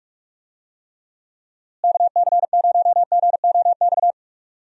Morse code, heard as a keyed tone, about 43 wpm, 700 Hz. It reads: KC0GOX
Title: USA Call Signs - 6 Characters - Repeated 3x - 45wpm